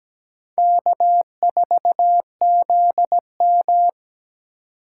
Morse code, heard as a keyed tone, about 17 wpm, 700 Hz. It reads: K4ZM
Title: USA Call Signs - 4 Characters - 3x Character Spacing - 17wpm